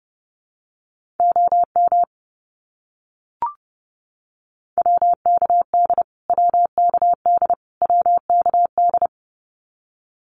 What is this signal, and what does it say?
Morse code, heard as a keyed tone, about 30 wpm, 700 Hz. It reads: OM WKD WKD WKD
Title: Abbreviations Repeated 3x - 30wpm